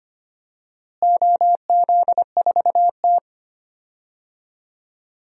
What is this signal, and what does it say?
Morse code, heard as a keyed tone, about 25 wpm, 700 Hz. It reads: OZ4T
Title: DXCC-100 Call Signs - 4 Characters - Repeated 2x - 25wpm